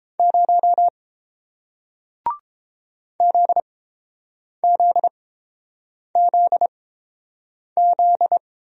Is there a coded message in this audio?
Yes: Morse code from 0.2 to 8.4 s, heard as a keyed tone, about 30 wpm, spaced as for 7 wpm, 700 Hz. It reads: 0 ZZZZ